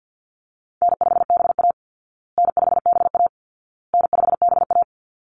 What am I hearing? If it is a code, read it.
Morse code, heard as a keyed tone, about 50 wpm, 700 Hz. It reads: N5BA N5BA N5BA
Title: Easy Call Signs Repeated 3x - 50wpm